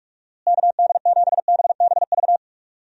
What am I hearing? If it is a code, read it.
Morse code, heard as a keyed tone, about 42 wpm, 700 Hz. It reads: KD7BBV
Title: USA Call Signs - 6 Characters - Repeated 2x - 45wpm